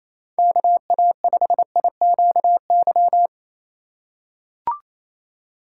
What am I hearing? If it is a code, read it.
Morse code, heard as a keyed tone, about 28 wpm, 700 Hz. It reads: KA5IQY
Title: USA Call Signs - 6 Characters - 2x Character Spacing - 28wpm